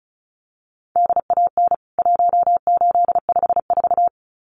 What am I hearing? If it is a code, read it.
Morse code, heard as a keyed tone, about 34 wpm, 700 Hz. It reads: DAN 1854
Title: CWT Exchange 35wpm